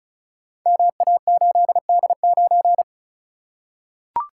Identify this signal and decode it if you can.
Morse code, heard as a keyed tone, about 36 wpm, 700 Hz. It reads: MA8D9
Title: Code Groups - 5 Letter:Numbers - 35wpm